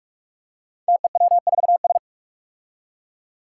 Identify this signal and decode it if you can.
Morse code, heard as a keyed tone, about 46 wpm, 700 Hz. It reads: TEW4S
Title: Five Challenging Letter-Number Combinations 45wpm